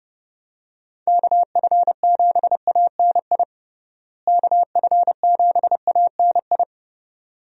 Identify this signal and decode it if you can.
Morse code, heard as a keyed tone, about 30 wpm, 700 Hz. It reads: KF7ANI KF7ANI
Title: USA Call Signs - 6 Characters - Repeated 2x - 30wpm